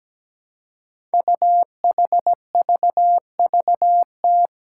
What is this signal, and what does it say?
Morse code, heard as a keyed tone, about 17 wpm, 700 Hz. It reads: UHVVT
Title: Five Challenging Letter-Number Combinations 17wpm